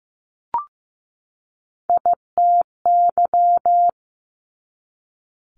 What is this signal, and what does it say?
Morse code, heard as a keyed tone, about 15 wpm, 700 Hz. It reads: ITY
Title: Multi-letter Phonemes 15wpm